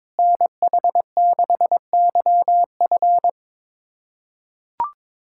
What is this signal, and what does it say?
Morse code, heard as a keyed tone, about 22 wpm, 700 Hz. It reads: NH6YF
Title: USA Callsign DE Callsign - 5 Characters - 22wpm